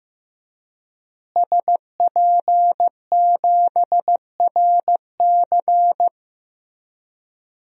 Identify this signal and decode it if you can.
Morse code, heard as a keyed tone, about 15 wpm, 700 Hz. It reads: SP7RC
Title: DXCC-100 Call Signs - 5 Characters - Repeated 2x - 15wpm